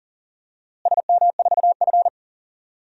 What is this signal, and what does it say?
Morse code, heard as a keyed tone, about 37 wpm, 700 Hz. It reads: SM4F
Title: DXCC-100 Call Signs - 4 Characters - Repeated 3x - 40wpm